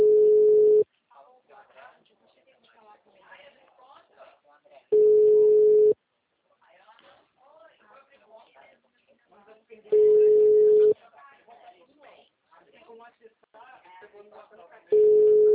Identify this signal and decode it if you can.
Ringback tone (Europe, 425 Hz), with speech.